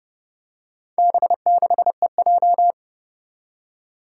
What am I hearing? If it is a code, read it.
Morse code, heard as a keyed tone, about 30 wpm, 700 Hz. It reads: B6EJ
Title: Four Challenging Letter-Number Combinations 30wpm